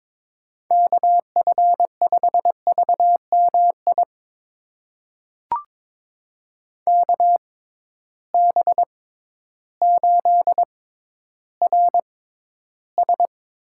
Morse code, heard as a keyed tone, about 22 wpm, spaced as for 12 wpm, 700 Hz. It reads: KF5VMI KB8RS
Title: USA Call Signs - 6 Characters - 6x Character Spacing - 22wpm